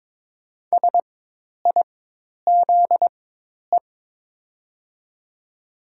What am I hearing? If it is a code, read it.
Morse code, heard as a keyed tone, about 22 wpm, spaced as for 10 wpm, 700 Hz. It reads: SIZE
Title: 4x Character Spacing - Top 500 Words - 22wpm